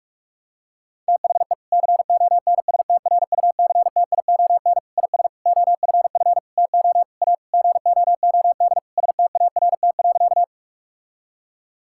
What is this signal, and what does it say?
Morse code, heard as a keyed tone, about 46 wpm, 700 Hz. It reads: THE CONSTRUCTION IS OFF TO A GOOD START.
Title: Sentences from Top 1600 Words 45wpm